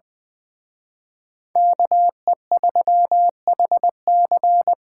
Morse code, heard as a keyed tone, about 24 wpm, 700 Hz. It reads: KE3HC
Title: Speed Racing - USA 2x2 Call Signs 20wpm